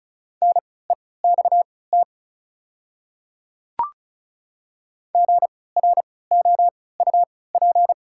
Morse code, heard as a keyed tone, about 36 wpm, spaced as for 20 wpm, 700 Hz. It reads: NEXT GROUP
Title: 3x Character Spacing - Top 300 Words - 35wpm